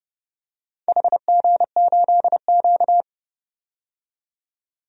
Morse code, heard as a keyed tone, about 28 wpm, 700 Hz. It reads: HG8Q
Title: DXCC-100 Call Signs - 4 Characters - 30wpm